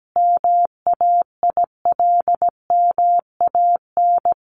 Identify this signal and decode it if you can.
Morse code, heard as a keyed tone, about 17 wpm, 700 Hz. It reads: MAILMAN
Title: Additional Easy Phrases 17wpm